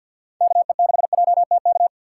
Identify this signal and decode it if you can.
Morse code, heard as a keyed tone, about 51 wpm, 700 Hz. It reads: KE6JTK
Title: USA Call Signs - 6 Characters - 4x Character Spacing - 50wpm